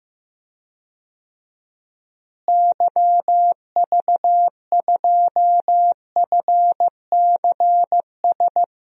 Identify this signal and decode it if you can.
Morse code, heard as a keyed tone, about 15 wpm, 700 Hz. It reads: YV2FCS
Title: DXCC-100 Call Signs - 6 Characters - Repeated 2x - 15wpm